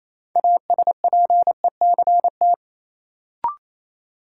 Morse code, heard as a keyed tone, about 28 wpm, 700 Hz. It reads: ASPECT
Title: Sentences from Top 1300 Words - New Word Drill - 28wpm